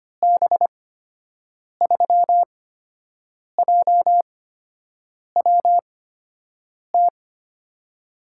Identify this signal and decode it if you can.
Morse code, heard as a keyed tone, about 25 wpm, spaced as for 7 wpm, 700 Hz. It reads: B3JWT
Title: USA Call Signs - 6 Characters - 8x Character Spacing - 25wpm